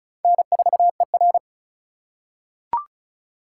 Morse code, heard as a keyed tone, about 34 wpm, 700 Hz. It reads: N4ER
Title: USA Call Signs - 4 Characters - 3x Character Spacing - 35wpm